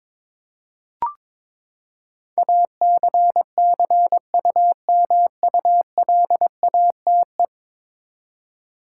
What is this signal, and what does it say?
Morse code, heard as a keyed tone, about 22 wpm, 700 Hz. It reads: ACCUMULATE
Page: morse-code-ninja-practice-22wpm